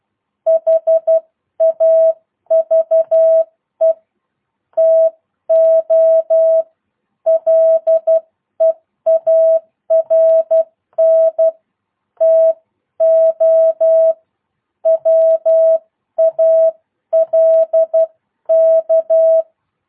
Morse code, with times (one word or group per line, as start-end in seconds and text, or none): 0.5-3.9 HAVE
4.8-6.6 TO
7.3-11.5 LEARN
12.2-14.1 TO
14.9-19.4 WALK